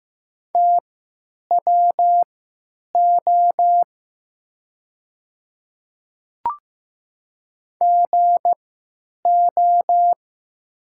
Morse code, heard as a keyed tone, about 15 wpm, spaced as for 9 wpm, 700 Hz. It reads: TWO GO
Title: 3x Character Spacing - Top 100 Words - 15wpm